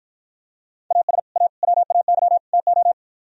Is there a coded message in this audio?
Yes: Morse code from 0.9 to 2.9 s, heard as a keyed tone, about 52 wpm, 700 Hz. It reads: AS A WAY TO